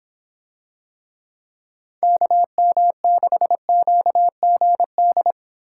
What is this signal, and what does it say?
Morse code, heard as a keyed tone, about 26 wpm, 700 Hz. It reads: KM6QGD